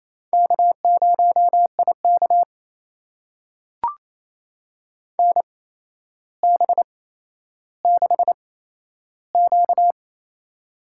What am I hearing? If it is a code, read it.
Morse code, heard as a keyed tone, about 27 wpm, spaced as for 12 wpm, 700 Hz. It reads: K0IK NB6Q
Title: USA Call Signs - 4 Characters - 8x Character Spacing - 28wpm